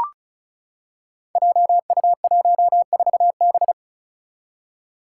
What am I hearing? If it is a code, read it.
Morse code, heard as a keyed tone, about 36 wpm, 700 Hz. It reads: JU14B
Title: Five Challenging Letter-Number Combinations 35wpm